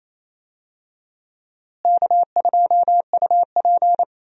Morse code, heard as a keyed tone, about 28 wpm, 700 Hz. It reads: K2UP
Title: USA Call Signs - 4 Characters - Repeated 3x - 28wpm